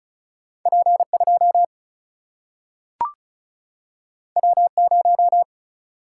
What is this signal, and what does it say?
Morse code, heard as a keyed tone, about 36 wpm, 700 Hz. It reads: P 2 W 0
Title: Single-Letter Single-Number 35wpm